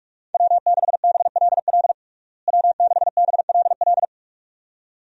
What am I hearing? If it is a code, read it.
Morse code, heard as a keyed tone, about 46 wpm, 700 Hz. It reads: W6BLL W6BLL
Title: USA Call Signs - 5 Characters - Repeated 2x - 45wpm